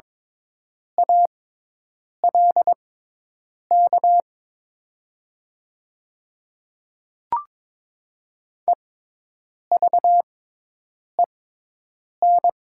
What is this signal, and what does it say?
Morse code, heard as a keyed tone, about 22 wpm, spaced as for 8 wpm, 700 Hz. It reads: ALK EVEN
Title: 6x Character Spacing - Top 300 Words - 22wpm